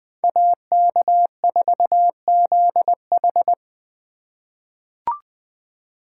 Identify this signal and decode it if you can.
Morse code, heard as a keyed tone, about 20 wpm, 700 Hz. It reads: AK4ZH
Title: USA Call Signs - 3x Character Spacing - 20wpm